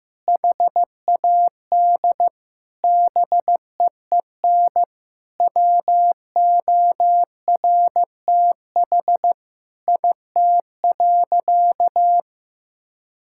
Morse code, heard as a keyed tone, about 15 wpm, 700 Hz. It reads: HAD BEEN WORTH IT.
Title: Sentences from Top 1000 Words - Review Part 5 - 15wpm